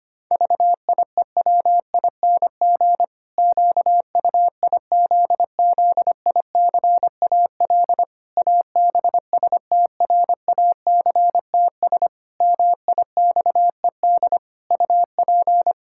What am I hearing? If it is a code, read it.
Morse code, heard as a keyed tone, about 25 wpm, 700 Hz. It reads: VIEWING QUIZZICAL ABSTRACTS MIXED UP
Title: Pangram Challenge v2 - 25wpm